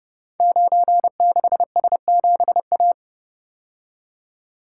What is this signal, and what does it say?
Morse code, heard as a keyed tone, about 30 wpm, 700 Hz. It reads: 96S7A
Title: Code Groups - 5 Letter:Numbers - 30wpm